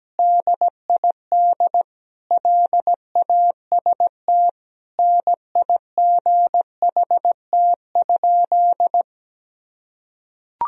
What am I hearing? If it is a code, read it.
Morse code, heard as a keyed tone, about 17 wpm, 700 Hz. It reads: DID LAST NIGHT?
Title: Sentences from Top 300 Words and Abbreviations 17wpm